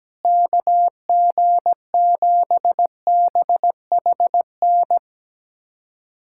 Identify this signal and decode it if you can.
Morse code, heard as a keyed tone, about 17 wpm, 700 Hz. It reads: KG7BHN
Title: USA Callsign DE Callsign - 6 Characters - 17wpm